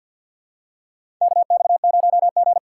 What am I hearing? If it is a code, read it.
Morse code, heard as a keyed tone, about 52 wpm, 690 Hz. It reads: KX0G
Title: USA Call Signs - 2x Character Spacing - 50wpm